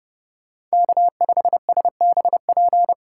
Morse code, heard as a keyed tone, about 36 wpm, 700 Hz. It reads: K5SBP
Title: Speed Racing - USA Call Signs 30wpm